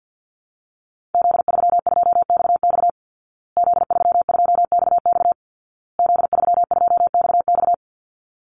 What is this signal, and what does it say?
Morse code, heard as a keyed tone, about 50 wpm, 700 Hz. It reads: Z32XX Z32XX Z32XX
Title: Call Signs Repeated 2x - 50wpm